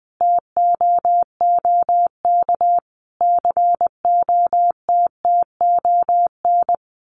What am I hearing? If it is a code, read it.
Morse code, heard as a keyed tone, about 20 wpm, 700 Hz. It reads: TOOK COTTON
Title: A Sound Of Thunder 20wpm